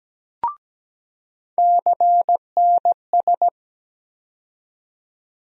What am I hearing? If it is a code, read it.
Morse code, heard as a keyed tone, about 17 wpm, 700 Hz. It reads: CNS
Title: USA Call Sign Postfixes - 3 Letters 17wpm